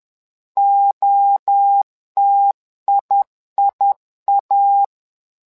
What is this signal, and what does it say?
Morse code, heard as a keyed tone, about 11 wpm, 800 Hz. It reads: OTIIA
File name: LEZIONI / 3 MWRO / 3 MWRO 10wpm 2148